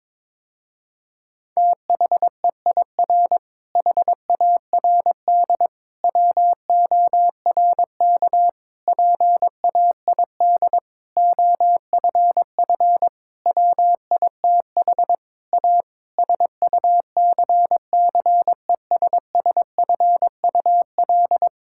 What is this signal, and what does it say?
Morse code, heard as a keyed tone, about 22 wpm, 700 Hz. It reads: THEIR HARD WORK PAID OFF WITH A SUCCESSFUL
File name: Difficult Sentences with Hints - 22wpm